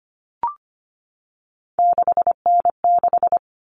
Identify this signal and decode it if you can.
Morse code, heard as a keyed tone, about 25 wpm, 700 Hz. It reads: 6N6
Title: All Cut Numbers - 3-Digit 25wpm